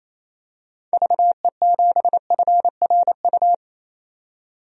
Morse code, heard as a keyed tone, about 27 wpm, 700 Hz. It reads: VE7FRU
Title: DXCC-100 Call Signs - 6 Characters - 28wpm